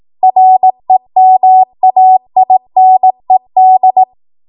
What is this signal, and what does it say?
Morse code, heard as a keyed tone, about 18 wpm, 750 Hz. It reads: REMAINED
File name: Morse / 18WPM